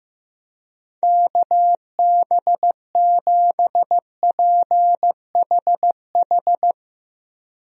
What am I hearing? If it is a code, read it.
Morse code, heard as a keyed tone, about 16 wpm, 700 Hz. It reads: KB7PHH
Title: USA Call Signs - 6 Characters - Repeated 3x - 15wpm